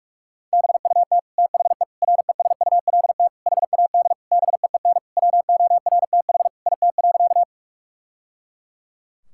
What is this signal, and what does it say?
Morse code, heard as a keyed tone, about 46 wpm, 700 Hz. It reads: BUT THE RESULT HAD BEEN WORTH IT.
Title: Sentences from Top 1000 Words - Review Part 5 - 45wpm